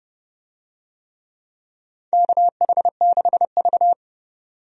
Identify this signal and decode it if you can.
Morse code, heard as a keyed tone, about 30 wpm, 700 Hz. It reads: KH6V
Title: USA Call Signs - 4 Characters - 30wpm